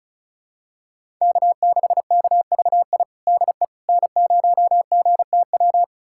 Morse code, heard as a keyed tone, about 36 wpm, 700 Hz. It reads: K6KVI DE N0GTW
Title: USA Callsign DE Callsign - 5 Characters - 35wpm